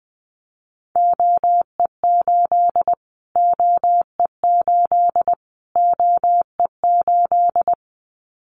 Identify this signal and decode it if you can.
Morse code, heard as a keyed tone, about 20 wpm, 700 Hz. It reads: OE8 OE8 OE8
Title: Callsign Prefixes for Top 25 DXCC - Repeated 3x - 20wpm